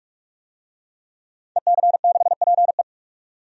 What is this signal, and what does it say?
Morse code, heard as a keyed tone, about 46 wpm, 700 Hz. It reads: EC6PE
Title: DXCC-100 Call Signs - 5 Characters - 45wpm